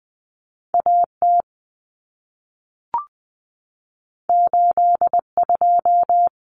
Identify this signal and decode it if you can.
Morse code, heard as a keyed tone, about 20 wpm, 700 Hz. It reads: A T 8 2
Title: Common Cut Numbers - 2-Digit 20wpm